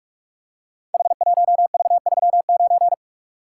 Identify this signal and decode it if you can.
Morse code, heard as a keyed tone, about 44 wpm, 700 Hz. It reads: H1V39
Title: Code Groups - 5 Letter:Numbers - 45wpm